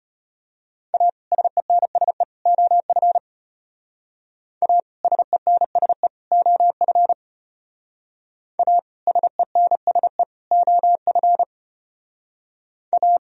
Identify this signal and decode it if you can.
Morse code, heard as a keyed tone, about 31 wpm, 700 Hz. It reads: A SENSE OF A SENSE OF A SENSE OF A
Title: Speed-Racing - Sets of 3 Words 25wpm